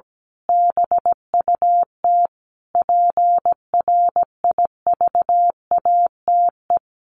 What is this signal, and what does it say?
Morse code, heard as a keyed tone, about 17 wpm, 700 Hz. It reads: BUT PRIVATE
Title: Sentences from Top 1200 Words 17wpm